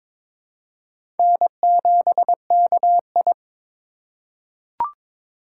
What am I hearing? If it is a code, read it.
Morse code, heard as a keyed tone, about 22 wpm, 700 Hz. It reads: N7KI
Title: USA Call Signs - 4 Characters - 4x Character Spacing - 22wpm